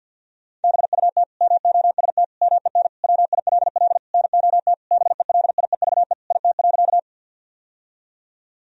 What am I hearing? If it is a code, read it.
Morse code, heard as a keyed tone, about 50 wpm, 700 Hz. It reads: BUT MOST MEN WILL NOT BELIEVE IT.
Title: Sentences from Top 1000 Words - Review Part 1 - 50wpm